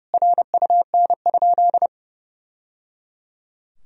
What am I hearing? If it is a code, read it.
Morse code, heard as a keyed tone, about 30 wpm, 700 Hz. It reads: RUN?
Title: Sentences from Top 1000 Words - Review Part 2 - 30wpm